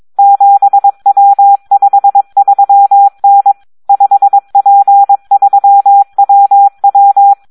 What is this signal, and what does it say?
Morse code, heard as a keyed tone, about 22 wpm, 800 Hz. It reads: 7W53N 5P3WW
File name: Audio practica CW / SD 1 22